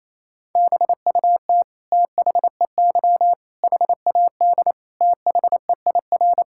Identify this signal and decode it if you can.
Morse code, heard as a keyed tone, about 28 wpm, 700 Hz. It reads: BUT THEY HAD THEIR
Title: Sentences from Top 1000 Words - Review Part 6 - 28wpm